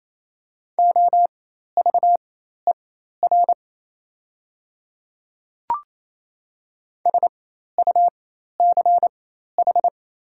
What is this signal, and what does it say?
Morse code, heard as a keyed tone, about 28 wpm, spaced as for 13 wpm, 700 Hz. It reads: OVER SUCH
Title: 4x Character Spacing - Top 300 Words - 28wpm